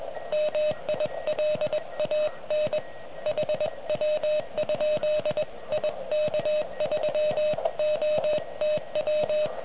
Morse code, heard as a keyed tone, about 20 wpm, 610 Hz, with noise. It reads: MILAN HW? IK3OTW